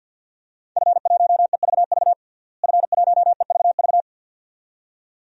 Morse code, heard as a keyed tone, about 50 wpm, 700 Hz. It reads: F1EVV F1EVV
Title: DXCC-100 Call Signs - 5 Characters - Repeated 2x - 50wpm